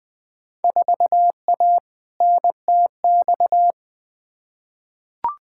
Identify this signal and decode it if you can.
Morse code, heard as a keyed tone, about 20 wpm, 700 Hz. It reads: 4A NTX
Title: Field Day Exchanges - 20wpm